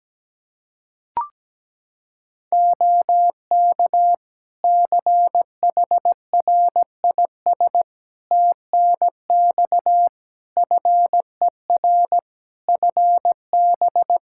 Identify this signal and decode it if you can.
Morse code, heard as a keyed tone, about 17 wpm, 700 Hz. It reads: OK CHRIS TNX FER FB
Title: Practice QSO - Exchange 4 - Ending - 17wpm